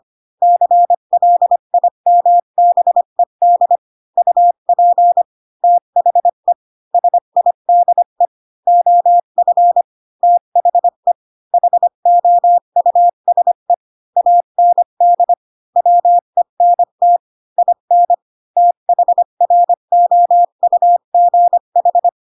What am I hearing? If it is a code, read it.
Morse code, heard as a keyed tone, about 25 wpm, 700 Hz. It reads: CLIMBED UP THE SIDE OF THE HOUSE AND WENT IN THROUGH
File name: Pinocchio / Pinocchio ch7 20 25 ik1hge